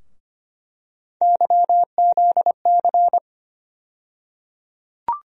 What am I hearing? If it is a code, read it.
Morse code, heard as a keyed tone, about 25 wpm, 700 Hz. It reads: YZC
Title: USA Call Sign Postfixes - 3 Letters 25wpm